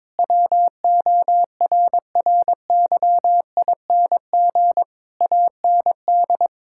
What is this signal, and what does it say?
Morse code, heard as a keyed tone, about 22 wpm, 700 Hz. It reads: WORRYING AND
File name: practice / 22wpm / Sets of 8 Words - 22wpm